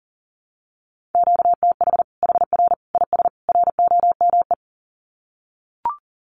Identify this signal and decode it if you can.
Morse code, heard as a keyed tone, about 40 wpm, 700 Hz. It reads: QTH HR IS ROME
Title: Common QSO Phrases 40wpm